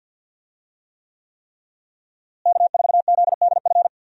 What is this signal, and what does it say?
Morse code, heard as a keyed tone, about 52 wpm, 700 Hz. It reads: K4ZDF
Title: USA Call Signs - 5 Characters - Repeated 3x - 50wpm